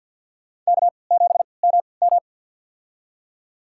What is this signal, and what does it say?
Morse code, heard as a keyed tone, about 52 wpm, spaced as for 29 wpm, 700 Hz. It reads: K7MM
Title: USA Call Signs - 3x Character Spacing - 50wpm